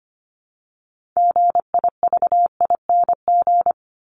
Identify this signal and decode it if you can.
Morse code, heard as a keyed tone, about 25 wpm, 700 Hz. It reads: GIVING
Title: ING Suffix - Top 100 Words with Base Words 25wpm